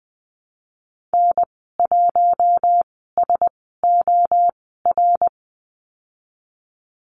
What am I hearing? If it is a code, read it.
Morse code, heard as a keyed tone, about 20 wpm, spaced as for 14 wpm, 700 Hz. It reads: N1SOR
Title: 122 Lesson 20g - Number 3 - Call Signs 20wpm